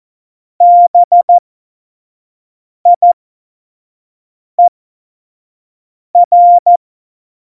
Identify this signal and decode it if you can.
Morse code, heard as a keyed tone, about 14 wpm, spaced as for 5 wpm, 700 Hz. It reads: BIER
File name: Morsekurs / Witze / Witz 19 Mai 2020, 25 F70